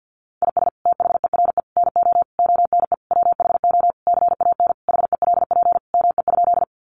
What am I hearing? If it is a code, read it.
Morse code, heard as a keyed tone, about 50 wpm, 700 Hz. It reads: IS THERE NO ONE WHO CAN HELP ME?
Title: Sentences from Top 200 Words 50wpm